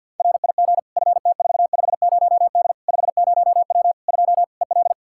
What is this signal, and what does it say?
Morse code, heard as a keyed tone, about 48 wpm, 700 Hz. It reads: RIG FT450D 50W 2 EL
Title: Practice QSO - Exchange 2 - Info - 50wpm